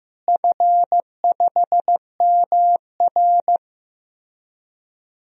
Morse code, heard as a keyed tone, about 15 wpm, 700 Hz. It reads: F5MR
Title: DXCC-100 Call Signs - 4 Characters - Repeated 3x - 15wpm